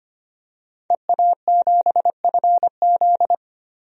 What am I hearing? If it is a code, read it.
Morse code, heard as a keyed tone, about 25 wpm, 700 Hz. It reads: EA7FZ
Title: DXCC-100 Call Signs - 5 Characters - 25wpm